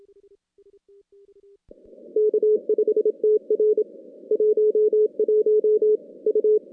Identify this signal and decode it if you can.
Morse code, heard as a keyed tone, about 26 wpm, 410 Hz. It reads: K5TR 11 U
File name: contests / CW-SS-k5tr- -2100